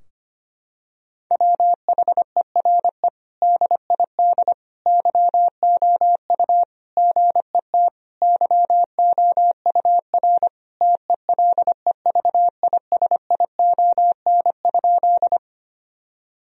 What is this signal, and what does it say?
Morse code, heard as a keyed tone, about 25 wpm, 700 Hz. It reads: WHERE DID YOU GET YOUR TELEVISION?
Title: Sentences from Top 1000 Words - Review Part 4 - 25wpm